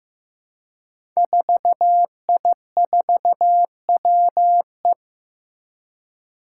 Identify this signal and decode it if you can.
Morse code, heard as a keyed tone, about 15 wpm, 700 Hz. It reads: 4I4WE
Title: Five Challenging Letter-Number Combinations 15wpm